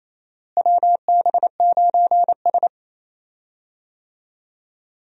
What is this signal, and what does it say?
Morse code, heard as a keyed tone, about 28 wpm, 700 Hz. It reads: WB9S
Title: USA Call Signs - 4 Characters - Repeated 2x - 28wpm